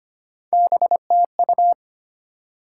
Morse code, heard as a keyed tone, about 34 wpm, spaced as for 30 wpm, 700 Hz. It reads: BTU
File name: practice / 25wpm / Speed-Racing Top 20 QSO Elements - 25wpm